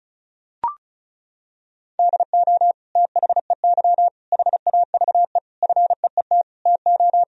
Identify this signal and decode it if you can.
Morse code, heard as a keyed tone, about 34 wpm, 700 Hz. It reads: DO THEY HAVE FEET TO
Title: Sentences from Top 1000 Words - Review Part 1 - 35wpm